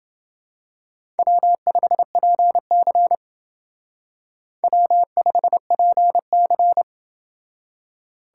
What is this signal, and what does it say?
Morse code, heard as a keyed tone, about 28 wpm, 700 Hz. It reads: W5PC W5PC